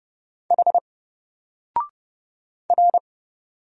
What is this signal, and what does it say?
Morse code, heard as a keyed tone, about 31 wpm, 700 Hz. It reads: H R
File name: Single-Letter-Number-30wpm